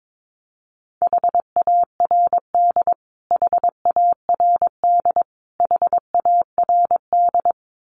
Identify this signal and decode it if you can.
Morse code, heard as a keyed tone, about 22 wpm, 700 Hz. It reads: HARD HARD HARD
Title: Top 300 Words Repeated 22wpm